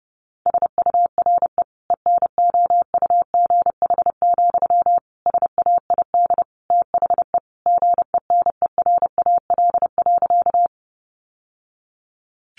Morse code, heard as a keyed tone, about 30 wpm, 700 Hz. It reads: SURE ENOUGH, SAID THE GENERAL.
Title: Sentences from Top 600 Words 30wpm